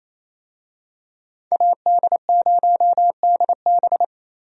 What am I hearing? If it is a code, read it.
Morse code, heard as a keyed tone, about 28 wpm, 700 Hz. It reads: AD0DB
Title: USA Call Signs - 5 Characters - Repeated 2x - 28wpm